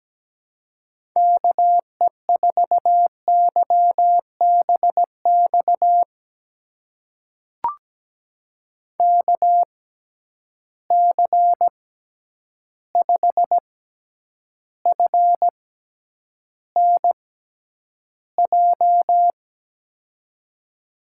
Morse code, heard as a keyed tone, about 17 wpm, spaced as for 9 wpm, 700 Hz. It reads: KE4YBX KC5FNJ
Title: USA Call Signs - 6 Characters - 6x Character Spacing - 17wpm